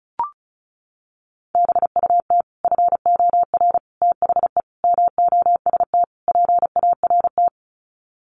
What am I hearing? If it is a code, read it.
Morse code, heard as a keyed tone, about 34 wpm, 700 Hz. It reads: BUT FOR THE MOST PART
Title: Sets of 5 Words Encore 35wpm